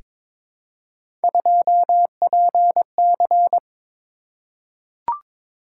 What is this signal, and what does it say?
Morse code, heard as a keyed tone, about 23 wpm, 700 Hz. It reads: 2PC
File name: practice / 22wpm / Three Letter-Number Combinations - 22wpm